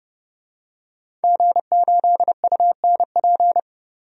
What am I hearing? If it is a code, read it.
Morse code, heard as a keyed tone, about 30 wpm, 700 Hz. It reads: G8UNP
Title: DXCC-100 Call Signs - 5 Characters - 30wpm